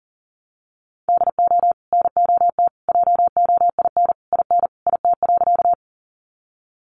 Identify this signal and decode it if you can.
Morse code, heard as a keyed tone, about 40 wpm, 700 Hz. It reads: DO NOT JOIN IN IT.
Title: Sentences from Top 1400 Words 40wpm